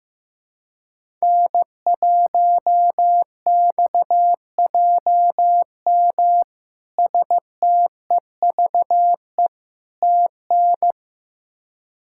Morse code, heard as a keyed tone, about 16 wpm, 700 Hz. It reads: N1XJM STEVE TN
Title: K1USN SST QSO Practice - 15wpm